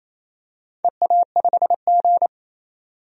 Morse code, heard as a keyed tone, about 27 wpm, 690 Hz. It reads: EA5G
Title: DXCC-100 Call Signs - 4 Characters - 28wpm